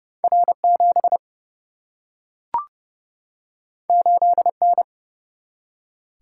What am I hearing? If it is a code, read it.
Morse code, heard as a keyed tone, about 30 wpm, 700 Hz. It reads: R7 8N
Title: Single-Letter Single-Number 30wpm